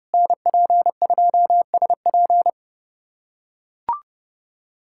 Morse code, heard as a keyed tone, about 30 wpm, 700 Hz. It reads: NP2SP
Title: USA Callsign DE Callsign - 5 Characters - 30wpm